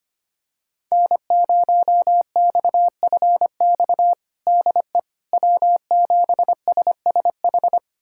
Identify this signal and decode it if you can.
Morse code, heard as a keyed tone, about 25 wpm, 700 Hz. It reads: N0XFX DE W7SSH
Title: USA Callsign DE Callsign - 5 Characters - 25wpm